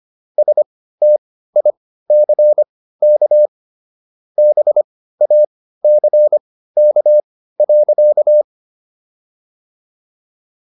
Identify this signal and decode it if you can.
Morse code, heard as a keyed tone, about 25 wpm, spaced as for 15 wpm, 600 Hz. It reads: STICK BACK.